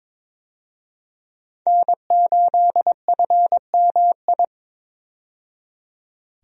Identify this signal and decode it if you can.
Morse code, heard as a keyed tone, about 22 wpm, 700 Hz. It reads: N8FMI
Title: USA Call Signs - 5 Characters - 22wpm